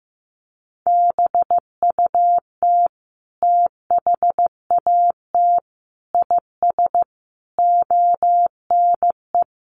Morse code, heard as a keyed tone, about 15 wpm, 700 Hz. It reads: BUT THAT IS ONE
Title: Sentences from Top 800 Words 15wpm